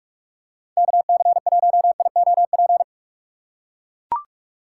Morse code, heard as a keyed tone, about 44 wpm, 700 Hz. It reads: KC1IOP
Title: USA Call Signs - 6 Characters - 6x Character Spacing - 45wpm